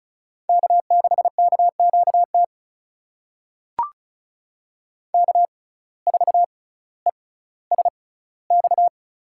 Morse code, heard as a keyed tone, about 35 wpm, spaced as for 18 wpm, 700 Hz. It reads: K6KQT K4ESX
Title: USA Call Signs - 5 Characters - 6x Character Spacing - 35wpm